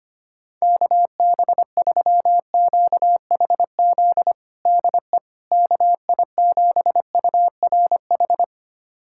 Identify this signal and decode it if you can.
Morse code, heard as a keyed tone, about 25 wpm, 700 Hz. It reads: KB3QHZ DE KI7URH
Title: USA Callsign DE Callsign - 6 Characters - 25wpm